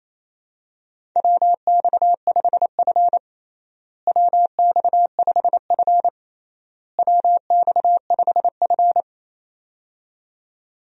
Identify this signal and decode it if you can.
Morse code, heard as a keyed tone, about 27 wpm, 700 Hz. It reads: WX5F WX5F WX5F
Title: USA Call Signs - 4 Characters - Repeated 3x - 28wpm